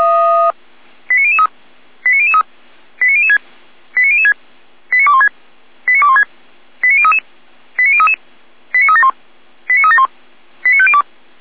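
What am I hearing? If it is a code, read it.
Five-tone selective call (ZVEI), heard as quick runs of five tones, tones of about 70 ms. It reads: 89003 89003 89006 89006 89216 89216 89033 89033 89371 89371 89502
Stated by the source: ZVEI five-tone selective call 89014, received by SDR.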